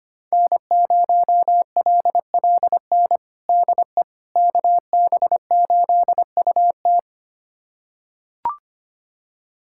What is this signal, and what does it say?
Morse code, heard as a keyed tone, about 25 wpm, 700 Hz. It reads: N0LLN DE KB8UT
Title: USA Callsign DE Callsign - 5 Characters - 25wpm